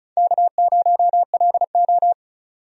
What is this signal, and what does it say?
Morse code, heard as a keyed tone, about 34 wpm, 700 Hz. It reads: K0LO
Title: USA Call Signs - 4 Characters - 35wpm